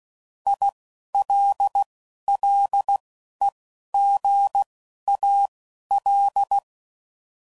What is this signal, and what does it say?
Morse code, heard as a keyed tone, about 16 wpm, 800 Hz. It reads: I L L E G A L
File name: Prefixes / By First Letter / Is